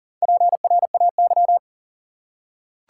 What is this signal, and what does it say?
Morse code, heard as a keyed tone, about 37 wpm, 700 Hz. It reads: PRAY
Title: Four Letter Words - 40wpm